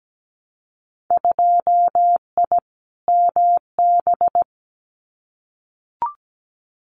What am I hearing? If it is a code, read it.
Morse code, heard as a keyed tone, about 17 wpm, 700 Hz. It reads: 2I MB
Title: Winter Field Day Exchanges Encore 17wpm